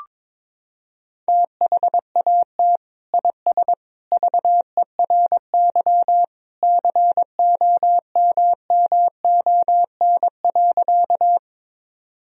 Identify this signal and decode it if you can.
Morse code, heard as a keyed tone, about 22 wpm, 700 Hz. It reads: THAT IS VERY COMMON.
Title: Sentences from Top 1000 Words - Review Part 3 - 22wpm